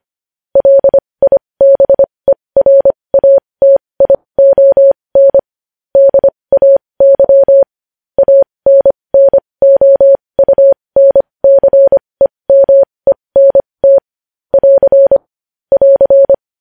Morse code, heard as a keyed tone, about 25 wpm, spaced as for 20 wpm, 550 Hz. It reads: LIBERATION DAY ANNOUNCEMENT <AR> <AR>
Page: news-headlines-in-morse-code-at-20-wpm